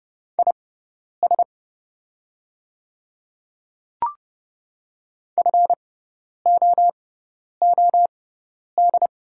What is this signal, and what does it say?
Morse code, heard as a keyed tone, about 30 wpm, spaced as for 10 wpm, 700 Hz. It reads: IS FOOD
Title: 6x Character Spacing - Top 500 Words - 30wpm